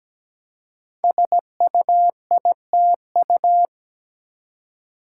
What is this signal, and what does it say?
Morse code, heard as a keyed tone, about 17 wpm, 700 Hz. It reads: SUITU
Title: Code Groups - 5 Letters - 17wpm